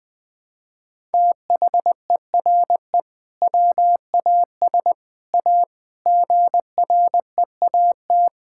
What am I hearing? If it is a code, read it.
Morse code, heard as a keyed tone, about 21 wpm, 700 Hz. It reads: THERE WAS A GREAT
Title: Sentences from Top 1600 Words 20wpm